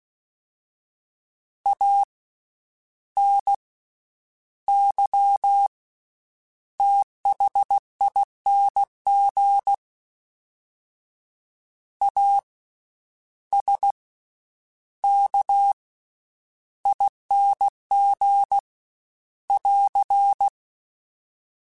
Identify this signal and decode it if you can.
Morse code, heard as a keyed tone, about 16 wpm, spaced as for 9 wpm, 800 Hz. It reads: ANYTHING ASKING <AR>
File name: Special INGs / Special INGs-mp3s / VerySlowToFaster Ing @15